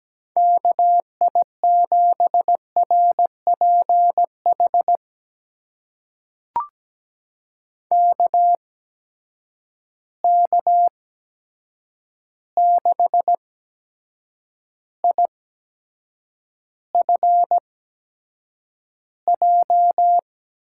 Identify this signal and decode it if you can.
Morse code, heard as a keyed tone, about 17 wpm, spaced as for 7 wpm, 700 Hz. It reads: KI7RPH KK6IFJ
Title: USA Call Signs - 6 Characters - 8x Character Spacing - 17wpm